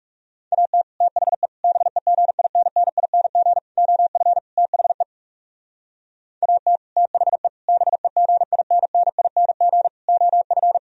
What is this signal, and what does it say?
Morse code, heard as a keyed tone, about 40 wpm, 700 Hz. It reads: AT THE BEGINNING OF THE AT THE BEGINNING OF